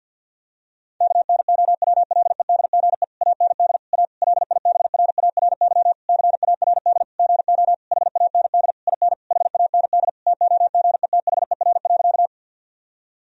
Morse code, heard as a keyed tone, about 53 wpm, 700 Hz. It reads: KNOWLEDGE AND A LIBRARY CARD GO HAND IN HAND TOGETHER.
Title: Sentences from Top 1000 Words - Review Part 4 - 50wpm